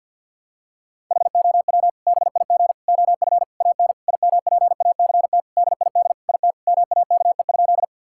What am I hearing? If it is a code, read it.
Morse code, heard as a keyed tone, about 51 wpm, 700 Hz. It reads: HOW BIG OF AN IMPACT DID IT MAKE?
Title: Sentences from Top 1000 Words - Review Part 6 - 50wpm